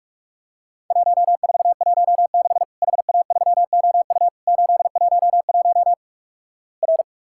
Morse code, heard as a keyed tone, about 46 wpm, 700 Hz. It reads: 1416 HA3OU 811 R
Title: Worked All Europe DX Contest - QTC Exchange - 45wpm